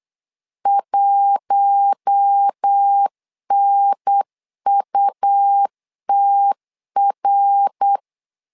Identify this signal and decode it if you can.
Morse code, heard as a keyed tone, about 8 wpm, 800 Hz. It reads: 1NUTR
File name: LEZIONI / 4 Q1 8wpm 3035